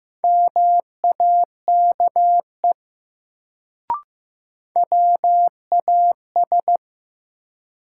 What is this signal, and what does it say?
Morse code, heard as a keyed tone, about 15 wpm, 700 Hz. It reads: MAKE WAS
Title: Sentences from Top 100 Words - New Word Drill - 15wpm